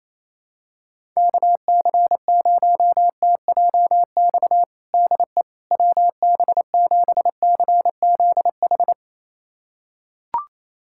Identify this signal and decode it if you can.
Morse code, heard as a keyed tone, about 28 wpm, 700 Hz. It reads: KC0TJX DE WB7CZH
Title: USA Callsign DE Callsign 28wpm